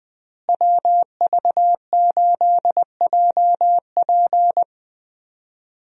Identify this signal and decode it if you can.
Morse code, heard as a keyed tone, about 21 wpm, 700 Hz. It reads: WV8JP
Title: USA Call Signs - 5 Characters - Repeated 3x - 20wpm